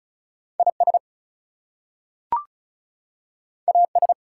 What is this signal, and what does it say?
Morse code, heard as a keyed tone, about 36 wpm, 700 Hz. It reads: IS AS
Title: Two Letter Words - 35wpm